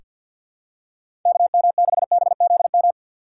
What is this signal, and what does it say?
Morse code, heard as a keyed tone, about 48 wpm, 700 Hz. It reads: KM6BZM
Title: USA Call Signs - 6 Characters - 6x Character Spacing - 50wpm